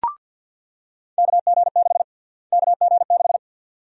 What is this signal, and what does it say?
Morse code, heard as a keyed tone, about 48 wpm, 700 Hz. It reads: KG6 KG6
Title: USA Call Sign Prefixes - Repeated 2x - 50wpm